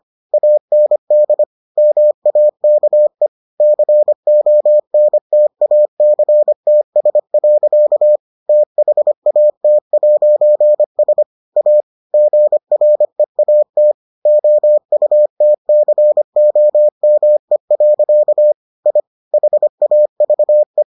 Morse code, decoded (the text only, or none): AND MAKE CONTACTS. THAT'S A GREAT OUTCOME. I HAVE